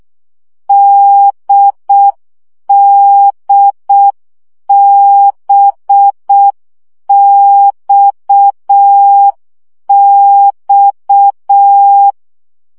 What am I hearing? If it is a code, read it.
Morse code, heard as a keyed tone, about 6 wpm, 800 Hz. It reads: DDBXX
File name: CW gyakorlas d b x 5 baud